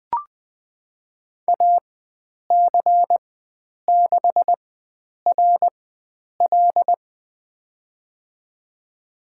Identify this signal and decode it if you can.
Morse code, heard as a keyed tone, about 20 wpm, spaced as for 9 wpm, 700 Hz. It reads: AC6RL
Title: USA Call Signs - 5 Characters - 4x Character Spacing - 20wpm